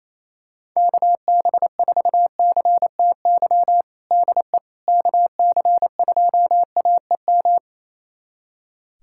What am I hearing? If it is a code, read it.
Morse code, heard as a keyed tone, about 28 wpm, 700 Hz. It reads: KB4CTY DE KC2AEM
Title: USA Callsign DE Callsign 28wpm